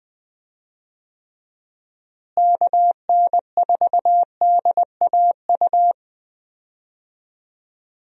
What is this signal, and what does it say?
Morse code, heard as a keyed tone, about 21 wpm, 700 Hz. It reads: KN4DAU
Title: USA Call Signs - 6 Characters - Repeated 2x - 20wpm